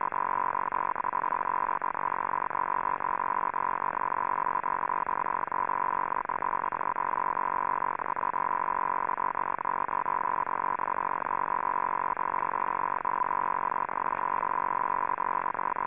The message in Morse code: none

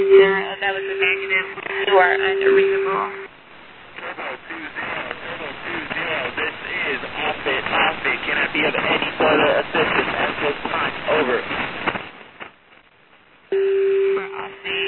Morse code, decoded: none

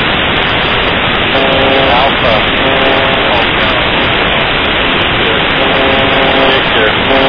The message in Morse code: none

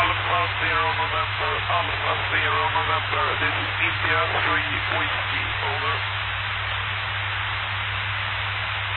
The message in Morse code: none

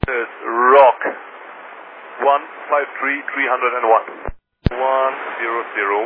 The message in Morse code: none

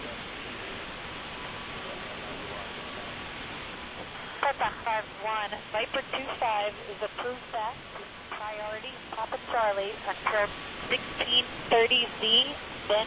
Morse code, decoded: none

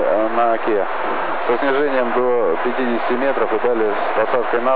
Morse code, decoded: none

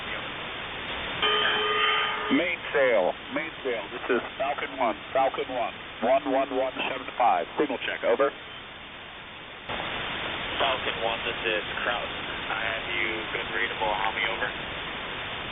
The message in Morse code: none

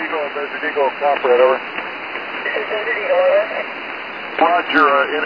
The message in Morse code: none

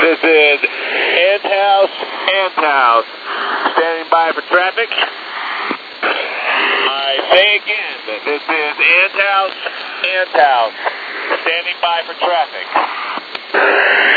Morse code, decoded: none

